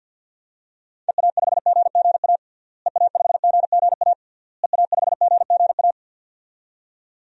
Morse code, heard as a keyed tone, about 48 wpm, 700 Hz. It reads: EA5GGA EA5GGA EA5GGA
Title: DXCC-100 Call Signs - 6 Characters - Repeated 3x - 50wpm